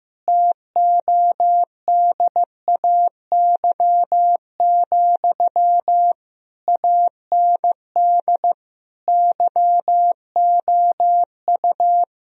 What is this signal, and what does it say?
Morse code, heard as a keyed tone, about 16 wpm, 700 Hz. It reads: TODAY, AND YOU
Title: Sentences from Top 1000 Words - Review Part 4 - 15wpm